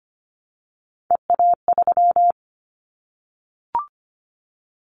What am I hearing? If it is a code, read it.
Morse code, heard as a keyed tone, about 25 wpm, 700 Hz. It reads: EA3
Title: Callsign Prefixes for Top 25 DXCC - Repeated 3x - 25wpm